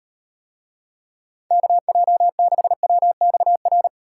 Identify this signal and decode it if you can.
Morse code, heard as a keyed tone, about 38 wpm, 700 Hz. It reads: KJ6WXR